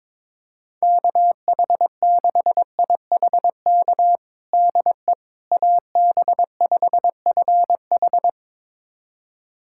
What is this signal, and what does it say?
Morse code, heard as a keyed tone, about 22 wpm, 700 Hz. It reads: KH6IHK DE AB5FH
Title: USA Callsign DE Callsign 22wpm